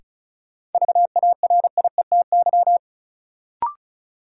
Morse code, heard as a keyed tone, about 36 wpm, 700 Hz. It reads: VARIETY
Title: Sentences from Top 1000 Words - Review Part 4 - New Word Drill - 35wpm